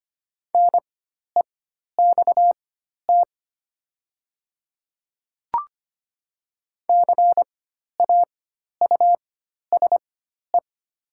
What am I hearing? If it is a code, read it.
Morse code, heard as a keyed tone, about 25 wpm, spaced as for 12 wpm, 700 Hz. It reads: NEXT CAUSE
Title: 4x Character Spacing - Top 500 Words - 25wpm